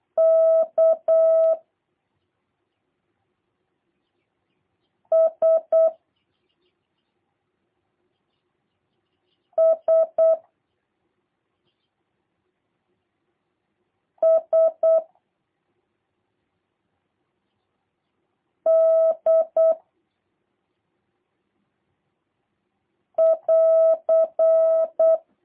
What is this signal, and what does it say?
Morse code, heard as a keyed tone, about 8 wpm, 650 Hz. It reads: K S S S D <AR>
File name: jh8cbh / 2023cw / review1-3